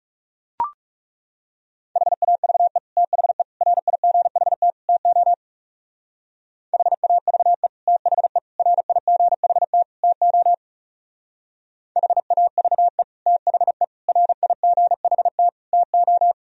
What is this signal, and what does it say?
Morse code, heard as a keyed tone, about 36 wpm, 700 Hz. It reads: HAVE THE RIGHT TO HAVE THE RIGHT TO HAVE THE RIGHT TO
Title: Speed-Racing - Sets of 4 Words 30wpm